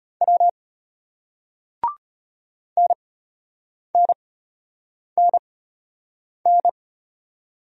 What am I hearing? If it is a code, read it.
Morse code, heard as a keyed tone, about 33 wpm, spaced as for 8 wpm, 700 Hz. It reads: W NNNN